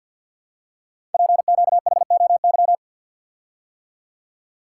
Morse code, heard as a keyed tone, about 48 wpm, 700 Hz. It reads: PQHOY
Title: Code Groups - 5 Letters - 50wpm